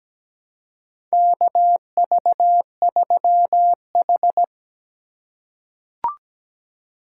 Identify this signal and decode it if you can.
Morse code, heard as a keyed tone, about 17 wpm, 700 Hz. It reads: KV3H
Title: USA Call Signs - 2x Character Spacing - 17wpm